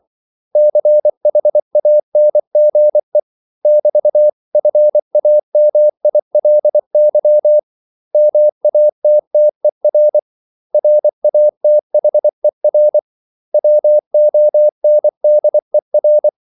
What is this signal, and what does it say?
Morse code, heard as a keyed tone, about 24 wpm, 600 Hz. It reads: CHANGE <BT>FAMILY MATTER RATHER WONDER
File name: CW / Practice / 500MostCommonEnglishWords24